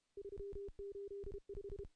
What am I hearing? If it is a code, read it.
Morse code, heard as a keyed tone, about 32 wpm, 400 Hz. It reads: 385
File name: contests / CW-SS-k5tr- -0100